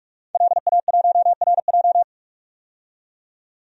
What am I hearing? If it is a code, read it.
Morse code, heard as a keyed tone, about 41 wpm, 700 Hz. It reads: LA1RJ
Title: DXCC-100 Call Signs - 5 Characters - 45wpm